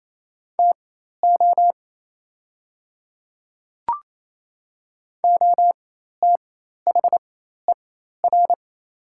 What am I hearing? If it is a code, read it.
Morse code, heard as a keyed tone, about 28 wpm, spaced as for 13 wpm, 700 Hz. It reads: TO OTHER
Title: 4x Character Spacing - Top 100 Words - 28wpm